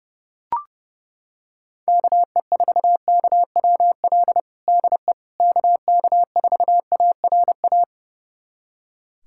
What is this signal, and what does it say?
Morse code, heard as a keyed tone, about 31 wpm, 700 Hz. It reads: KE4KWL DE KK4ARA
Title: USA Callsign DE Callsign - 6 Characters - 30wpm